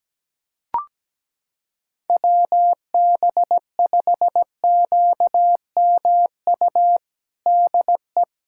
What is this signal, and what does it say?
Morse code, heard as a keyed tone, about 17 wpm, 700 Hz. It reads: WB5QMU DE
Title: USA Callsign DE Callsign 17wpm